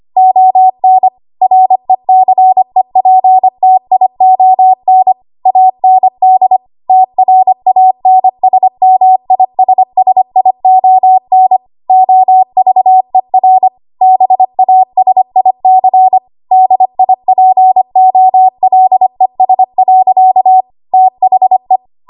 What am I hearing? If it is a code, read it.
Morse code, heard as a keyed tone, about 25 wpm, 750 Hz. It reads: ON RECEPTION AND TRANSMISSION OVER BASIC DIPOLES. THE
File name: Archive / 25 WPM / 25WPM